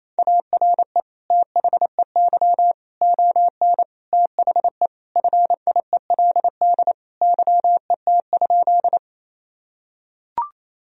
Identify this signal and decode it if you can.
Morse code, heard as a keyed tone, about 28 wpm, 700 Hz. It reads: ARE THEY ON THE FIELD YET?
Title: Sentences from Top 1000 Words - Review Part 2 - 28wpm